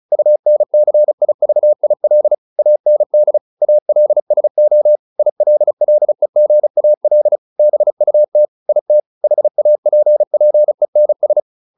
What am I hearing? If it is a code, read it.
Morse code, heard as a keyed tone, about 34 wpm, 600 Hz. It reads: UNCIVIL AND ALSO ILLEGAL BUT IT HAPPENS
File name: SS309 35